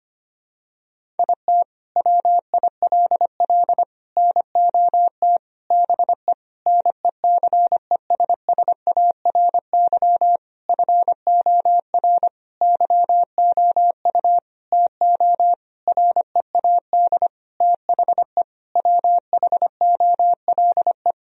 Morse code, heard as a keyed tone, about 25 wpm, 700 Hz. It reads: IT WILL NOT BE NECESSARY FOR YOU TO READ THE WHOLE
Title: Sentences from Top 1000 Words - Review Part 1 - 25wpm